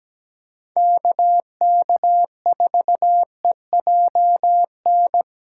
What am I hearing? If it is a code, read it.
Morse code, heard as a keyed tone, about 17 wpm, 700 Hz. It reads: KK4EJN
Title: USA Callsign DE Callsign 17wpm